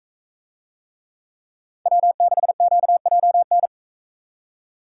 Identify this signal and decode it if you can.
Morse code, heard as a keyed tone, about 43 wpm, 700 Hz. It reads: W6QJN